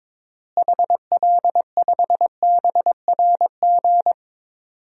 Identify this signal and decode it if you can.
Morse code, heard as a keyed tone, about 22 wpm, 700 Hz. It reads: HL5BRG
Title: DXCC-100 Call Signs - 6 Characters - Repeated 3x - 22wpm